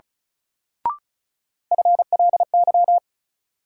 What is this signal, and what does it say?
Morse code, heard as a keyed tone, about 36 wpm, 700 Hz. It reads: FLY
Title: Sentences from Top 500 Words - New Word Drill - 35wpm